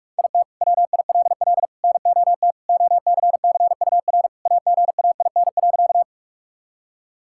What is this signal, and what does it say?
Morse code, heard as a keyed tone, about 42 wpm, 700 Hz. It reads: IT WILL NOT OCCUR AGAIN.
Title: Sentences from Top 1000 Words - Review Part 6 - 45wpm